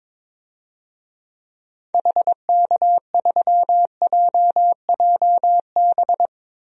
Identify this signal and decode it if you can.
Morse code, heard as a keyed tone, about 22 wpm, 700 Hz. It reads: HK3JJB
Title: DXCC-100 Call Signs - 6 Characters - Repeated 2x - 22wpm